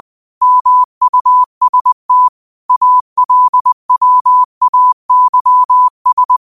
Morse code, heard as a keyed tone, about 20 wpm, 1000 Hz. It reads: MUST ALWAYS
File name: qotdfiles / 20wpm